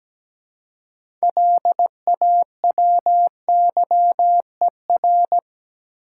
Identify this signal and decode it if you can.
Morse code, heard as a keyed tone, about 17 wpm, 700 Hz. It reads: LAWYER
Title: Six Letter Words - 17wpm